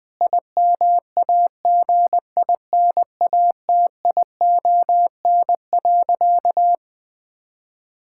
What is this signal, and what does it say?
Morse code, heard as a keyed tone, about 19 wpm, 700 Hz. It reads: IMAGINATION.
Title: Difficult Sentences with Hints - 20wpm